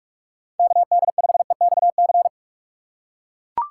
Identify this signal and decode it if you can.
Morse code, heard as a keyed tone, about 43 wpm, 690 Hz. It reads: KD5EXC
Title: USA Call Signs - 6 Characters - 3x Character Spacing - 45wpm